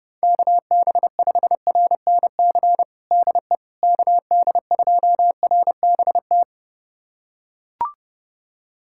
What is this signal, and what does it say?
Morse code, heard as a keyed tone, about 32 wpm, 700 Hz. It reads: KB5RNC DE KD2RBT
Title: USA Callsign DE Callsign - 6 Characters - 30wpm